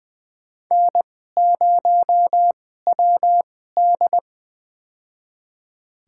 Morse code, heard as a keyed tone, about 20 wpm, spaced as for 15 wpm, 700 Hz. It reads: N0WD
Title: USA Call Signs - 4 Characters - 2x Character Spacing - 20wpm